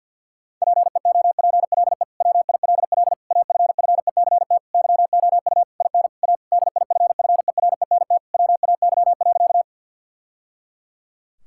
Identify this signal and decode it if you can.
Morse code, heard as a keyed tone, about 50 wpm, 700 Hz. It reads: PEOPLE WILL AFFECT YOU IN A DIFFERENT WAY.
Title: Sentences from Top 1000 Words - Review Part 6 - 50wpm